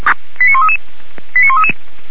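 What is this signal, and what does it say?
Five-tone selective call (ZVEI), heard as quick runs of five tones, tones of about 80 ms. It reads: 89133 89133
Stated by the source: Five-tone ZVEI selective call received on a software-defined radio.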